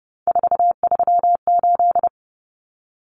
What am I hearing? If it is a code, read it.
Morse code, heard as a keyed tone, about 30 wpm, 700 Hz. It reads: 438
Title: Common Cut Numbers - 3-Digit 30wpm